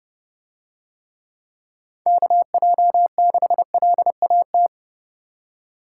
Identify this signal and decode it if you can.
Morse code, heard as a keyed tone, about 30 wpm, 700 Hz. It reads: KJ6LAT